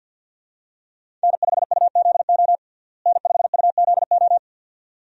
Morse code, heard as a keyed tone, about 50 wpm, 700 Hz. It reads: N5UZO N5UZO
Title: USA Call Signs - 5 Characters - Repeated 2x - 50wpm